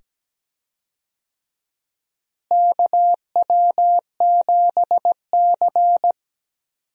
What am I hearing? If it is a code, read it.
Morse code, heard as a keyed tone, about 17 wpm, 700 Hz. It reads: KW7C
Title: USA Call Signs - 4 Characters - Repeated 2x - 17wpm